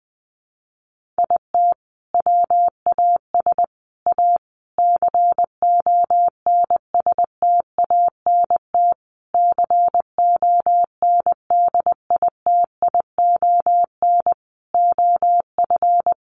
Morse code, heard as a keyed tone, about 20 wpm, 700 Hz. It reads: IT WAS A CONSTANT CONDITION OF
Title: Sentences from Top 1500 Words 20wpm